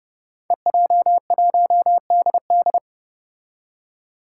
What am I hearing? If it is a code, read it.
Morse code, heard as a keyed tone, about 30 wpm, 700 Hz. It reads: EJ1DD
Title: Five Challenging Letter-Number Combinations 30wpm